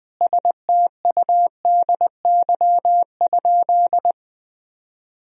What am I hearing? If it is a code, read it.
Morse code, heard as a keyed tone, about 20 wpm, 700 Hz. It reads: STUDY?
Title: Sentences from Top 1000 Words - Review Part 1 - 20wpm